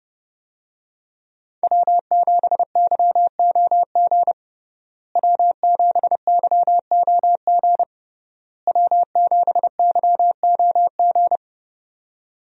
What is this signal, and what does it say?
Morse code, heard as a keyed tone, about 32 wpm, 700 Hz. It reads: W7YOG W7YOG W7YOG
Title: USA Call Signs - 5 Characters - Repeated 3x - 30wpm